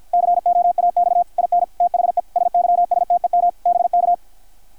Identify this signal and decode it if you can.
Morse code, heard as a keyed tone, about 53 wpm, 660 Hz, with radio noise. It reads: COAX IN THE SYSTEM BK
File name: archived / CRSnet